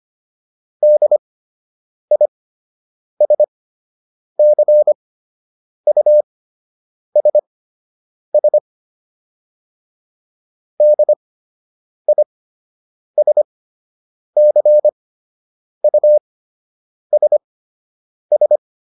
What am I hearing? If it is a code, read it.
Morse code, heard as a keyed tone, about 25 wpm, spaced as for 8 wpm, 600 Hz. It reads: DISCUSS DISCUSS